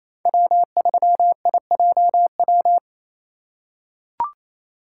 Morse code, heard as a keyed tone, about 28 wpm, 700 Hz. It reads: W3IJW
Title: USA Call Signs - 5 Characters - 4x Character Spacing - 28wpm